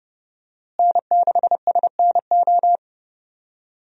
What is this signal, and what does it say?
Morse code, heard as a keyed tone, about 30 wpm, 700 Hz. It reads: N6SNO
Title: USA Call Signs - 5 Characters - Repeated 2x - 30wpm